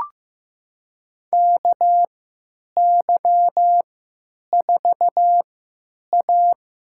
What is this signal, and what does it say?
Morse code, heard as a keyed tone, about 15 wpm, spaced as for 9 wpm, 700 Hz. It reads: KY4A
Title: USA Call Signs - 4 Characters - 3x Character Spacing - 15wpm